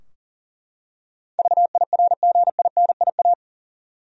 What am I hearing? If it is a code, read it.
Morse code, heard as a keyed tone, about 40 wpm, 700 Hz. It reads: VIRGINIA
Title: USA Common Names - 40wpm